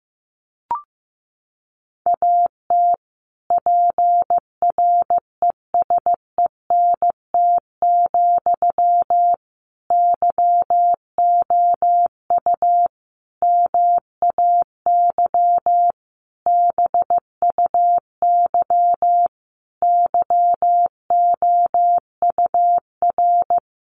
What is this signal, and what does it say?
Morse code, heard as a keyed tone, about 15 wpm, 700 Hz. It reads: AT PRESENT, YOU MAY BUY YOUR
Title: Sentences from Top 800 Words 15wpm